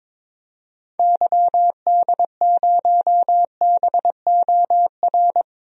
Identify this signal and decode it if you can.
Morse code, heard as a keyed tone, about 22 wpm, 700 Hz. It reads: YD0BOR
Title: DXCC-100 Call Signs - 6 Characters - 22wpm